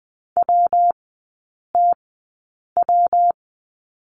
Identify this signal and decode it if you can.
Morse code, heard as a keyed tone, about 20 wpm, spaced as for 8 wpm, 700 Hz. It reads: WTW
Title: 178 Lesson 29e - Slash - Three Characters 20wpm